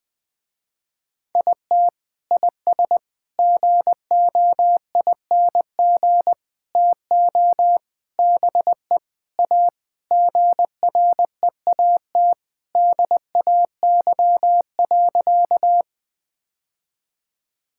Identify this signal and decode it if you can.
Morse code, heard as a keyed tone, about 20 wpm, 700 Hz. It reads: IT IS GOING TO BE A GREAT DAY.
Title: Sentences with 3-Word Phrases - Spoken Beginning - 20wpm